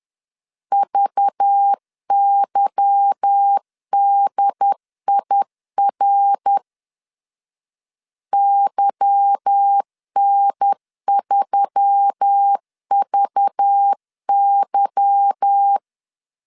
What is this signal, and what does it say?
Morse code, heard as a keyed tone, about 10 wpm, 800 Hz. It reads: VYDIR YN3VY
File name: LEZIONI / 8 YJ45 / 8 Y 10wpm 2615